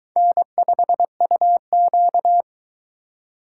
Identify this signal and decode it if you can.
Morse code, heard as a keyed tone, about 23 wpm, 700 Hz. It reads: N5UQ